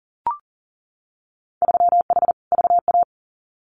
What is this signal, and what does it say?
Morse code, heard as a keyed tone, about 40 wpm, 700 Hz. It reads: 3H VA
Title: Winter Field Day Exchanges 40wpm